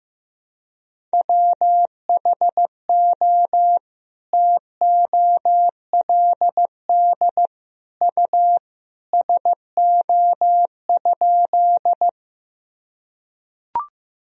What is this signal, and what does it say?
Morse code, heard as a keyed tone, about 15 wpm, 700 Hz. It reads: WHO TOLD U SO?
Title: Sentences from Top 400 Words and Abbreviations 15wpm